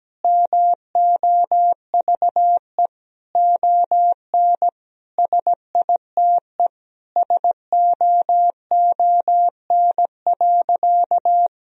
Morse code, heard as a keyed tone, about 17 wpm, 700 Hz. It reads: MOVE ON SITE SOON.
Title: Sentences from Top 1000 Words - Review Part 4 - 17wpm